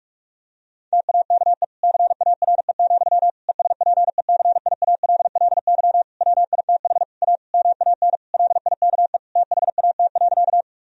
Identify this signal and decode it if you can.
Morse code, heard as a keyed tone, about 41 wpm, 700 Hz. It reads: TAKE CARE, ESPECIALLY WITH A MAN LIKE THAT.
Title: Sentences from Top 1000 Words - Review Part 4 - 45wpm